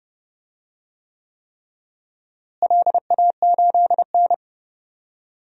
Morse code, heard as a keyed tone, about 30 wpm, 700 Hz. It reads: LA8N
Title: DXCC-100 Call Signs - 4 Characters - Repeated 3x - 30wpm